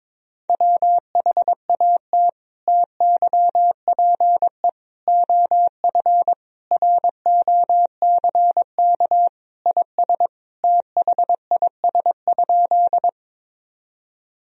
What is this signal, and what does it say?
Morse code, heard as a keyed tone, about 21 wpm, 700 Hz. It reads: WHAT TYPE OF ROCK IS THIS?
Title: Sentences from Top 1000 Words - Review Part 3 - 22wpm